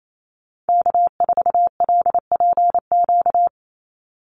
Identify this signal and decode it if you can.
Morse code, heard as a keyed tone, about 28 wpm, 700 Hz. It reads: K4LPQ
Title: NA SOTA Activator Callsigns 28wpm